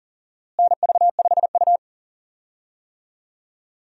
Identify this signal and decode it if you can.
Morse code, heard as a keyed tone, about 40 wpm, 700 Hz. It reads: NV5U
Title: USA Call Signs - 4 Characters - Repeated 3x - 40wpm